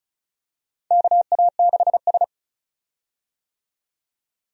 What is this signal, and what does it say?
Morse code, heard as a keyed tone, about 36 wpm, 700 Hz. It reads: KA6S
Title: USA Call Signs - 4 Characters - 35wpm